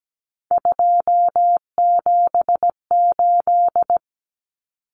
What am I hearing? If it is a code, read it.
Morse code, heard as a keyed tone, about 17 wpm, 700 Hz. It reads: 278
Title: Common Cut Numbers - 3-Digit 17wpm